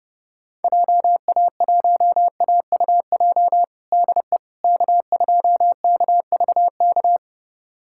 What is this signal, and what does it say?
Morse code, heard as a keyed tone, about 30 wpm, 700 Hz. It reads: JA1AUJ DE K2KVK
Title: USA Callsign DE Callsign 30wpm